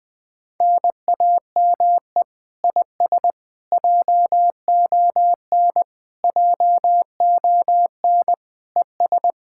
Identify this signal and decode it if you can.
Morse code, heard as a keyed tone, about 20 wpm, 700 Hz. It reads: NAME IS JON JON ES
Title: Speed-Racing Common QSO Phrases - 17wpm